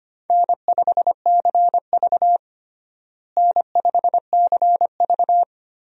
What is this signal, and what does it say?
Morse code, heard as a keyed tone, about 25 wpm, 700 Hz. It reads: N5CV N5CV
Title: USA Call Signs - 4 Characters - Repeated 3x - 25wpm